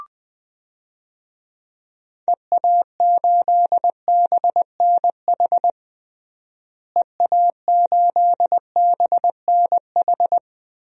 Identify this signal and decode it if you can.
Morse code, heard as a keyed tone, about 20 wpm, 700 Hz. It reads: EA8BNH EA8BNH
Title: DXCC-100 Call Signs - 6 Characters - Repeated 3x - 20wpm